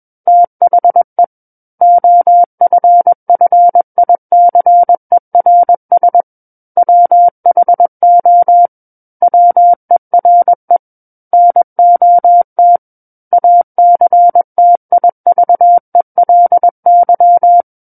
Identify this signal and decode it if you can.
Morse code, heard as a keyed tone, about 21 wpm, 700 Hz. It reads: THE OFFICERS WHO WERE NOT ACTIVELY
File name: Book One / 12-Chapter 12 - 21 WPM